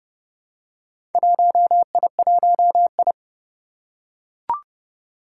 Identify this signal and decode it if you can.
Morse code, heard as a keyed tone, about 30 wpm, 700 Hz. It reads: 1I1I
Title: Four Challenging Letter-Number Combinations 30wpm